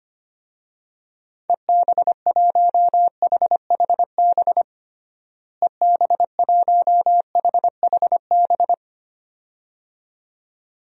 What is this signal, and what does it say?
Morse code, heard as a keyed tone, about 25 wpm, 700 Hz. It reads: EB1HHB EB1HHB
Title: DXCC-100 Call Signs - 6 Characters - Repeated 2x - 25wpm